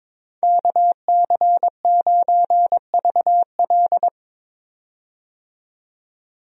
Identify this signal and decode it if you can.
Morse code, heard as a keyed tone, about 21 wpm, 700 Hz. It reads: KC9VL
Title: USA Call Signs - 5 Characters - 22wpm